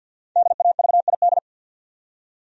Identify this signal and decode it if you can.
Morse code, heard as a keyed tone, about 49 wpm, 700 Hz. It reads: DAVID
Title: USA Common Names - Max 5 Letters - 50wpm